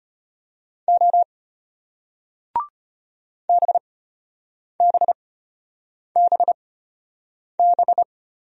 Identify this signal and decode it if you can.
Morse code, heard as a keyed tone, about 33 wpm, spaced as for 8 wpm, 700 Hz. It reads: O BBBB